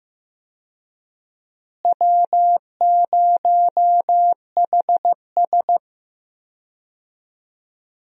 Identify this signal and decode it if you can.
Morse code, heard as a keyed tone, about 15 wpm, 700 Hz. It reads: W0HS
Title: USA Call Signs - 4 Characters - Repeated 2x - 15wpm